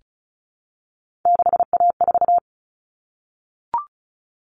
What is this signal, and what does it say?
Morse code, heard as a keyed tone, about 34 wpm, 700 Hz. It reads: 6A4
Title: All Cut Numbers - 3-Digit 35wpm